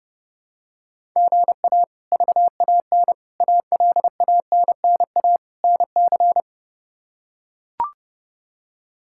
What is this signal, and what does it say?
Morse code, heard as a keyed tone, about 32 wpm, 700 Hz. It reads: GA VAN ALANNA NC
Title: K1USN SST QSO Practice - 30wpm